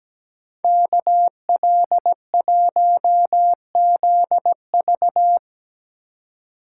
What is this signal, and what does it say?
Morse code, heard as a keyed tone, about 18 wpm, 700 Hz. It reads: KL1ZV
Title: Speed Racing - USA 2x2 Call Signs 17wpm